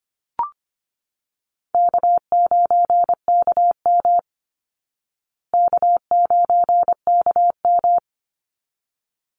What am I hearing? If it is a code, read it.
Morse code, heard as a keyed tone, about 25 wpm, 700 Hz. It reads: K9KM K9KM
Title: Call Signs Repeated 3x - 25wpm